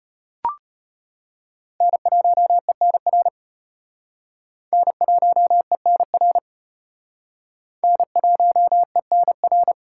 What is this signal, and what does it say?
Morse code, heard as a keyed tone, about 33 wpm, 700 Hz. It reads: N1ENR N1ENR N1ENR